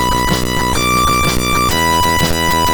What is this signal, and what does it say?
There is no code.